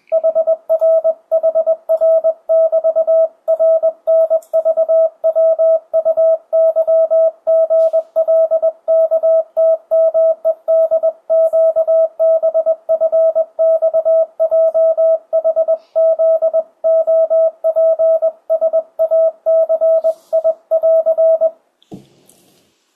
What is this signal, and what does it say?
Morse code, heard as a keyed tone, about 20 wpm, 650 Hz, with radio noise. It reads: HRHR<BT>RNVWUYGLKTMEDQBFXJHZOPSACI<AR>
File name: 2025cw / speedup / 21wpm